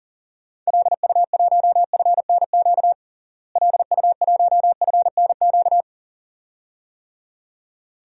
Morse code, heard as a keyed tone, about 40 wpm, 700 Hz. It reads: LU1FNQ LU1FNQ
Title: DXCC-100 Call Signs - 6 Characters - Repeated 2x - 40wpm